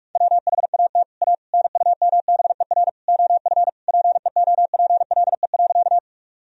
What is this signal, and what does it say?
Morse code, heard as a keyed tone, about 41 wpm, 700 Hz. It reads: WHAT A NUMBER OF PEOPLE.
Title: Sentences from Top 1000 Words - Review Part 1 - 45wpm